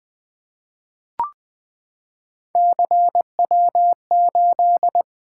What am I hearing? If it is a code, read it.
Morse code, heard as a keyed tone, about 20 wpm, 700 Hz. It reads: CW8
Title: Three Letter-Number Combinations - 20wpm